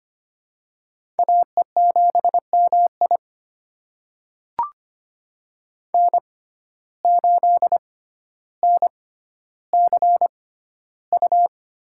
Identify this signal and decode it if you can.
Morse code, heard as a keyed tone, about 25 wpm, spaced as for 13 wpm, 700 Hz. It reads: AE7MI N8NCU
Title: USA Call Signs - 5 Characters - 6x Character Spacing - 25wpm